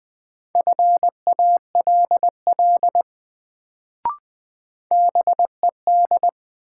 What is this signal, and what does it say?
Morse code, heard as a keyed tone, about 21 wpm, 700 Hz. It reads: FALL BED
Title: Sentences from Top 500 Words - New Word Drill - 20wpm